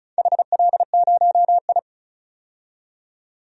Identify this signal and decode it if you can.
Morse code, heard as a keyed tone, about 34 wpm, 700 Hz. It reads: HL0I
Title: DXCC-100 Call Signs - 4 Characters - 35wpm